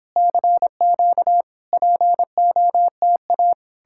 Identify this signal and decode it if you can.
Morse code, heard as a keyed tone, about 31 wpm, 700 Hz. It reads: CQ POTA